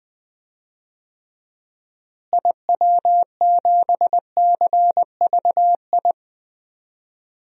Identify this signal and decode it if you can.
Morse code, heard as a keyed tone, about 20 wpm, 700 Hz. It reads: IW7CVI
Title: DXCC-100 Call Signs - 6 Characters - Repeated 3x - 20wpm